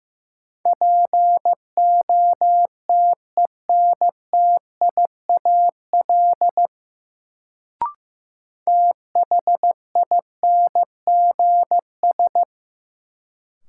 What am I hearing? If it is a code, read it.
Morse code, heard as a keyed tone, about 16 wpm, 700 Hz. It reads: POTENTIAL THINGS
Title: Sentences from Top 1000 Words - Review Part 4 - New Word Drill - 15wpm